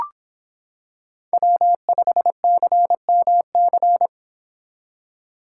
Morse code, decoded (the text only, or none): W5CMC